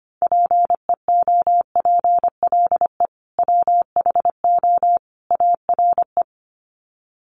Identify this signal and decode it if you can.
Morse code, heard as a keyed tone, about 25 wpm, 700 Hz. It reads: PEOPLE WHO ARE
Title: Sets of 3 Words 25wpm